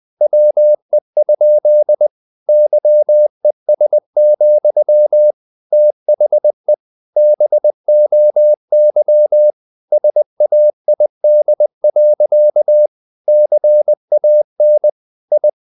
Morse code, heard as a keyed tone, about 20 wpm, 600 Hz. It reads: WE? YES, THE BOY SAID. CAN I
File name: the old man and the sea / the old man and the sea part 1 20wpm